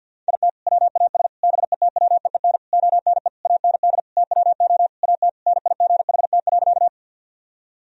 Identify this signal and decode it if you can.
Morse code, heard as a keyed tone, about 52 wpm, 700 Hz. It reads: IT WAS BETWEEN ONE AND TWO AT NIGHT.
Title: Sentences from Top 1000 Words - Review Part 1 - 50wpm